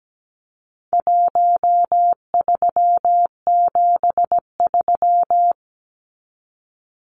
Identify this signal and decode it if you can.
Morse code, heard as a keyed tone, about 17 wpm, 700 Hz. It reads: 1373
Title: Numbers - Contest Style - 4-Digit 17wpm